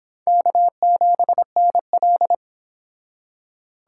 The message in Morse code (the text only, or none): K7NL